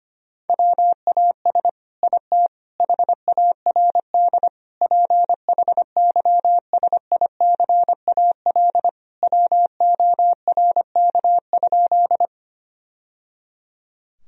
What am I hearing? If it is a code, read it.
Morse code, heard as a keyed tone, about 25 wpm, 700 Hz. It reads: WAS IT HARD PHYSICAL WORK?
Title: Sentences from Top 1000 Words - Review Part 4 - 25wpm